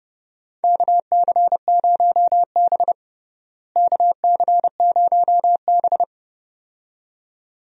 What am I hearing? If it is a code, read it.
Morse code, heard as a keyed tone, about 30 wpm, 700 Hz. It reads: KC0B KC0B
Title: USA Call Signs - 4 Characters - Repeated 2x - 30wpm